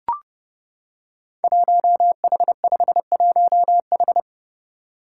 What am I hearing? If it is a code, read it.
Morse code, heard as a keyed tone, about 28 wpm, 700 Hz. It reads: 1H51H
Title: Five Challenging Letter-Number Combinations 30wpm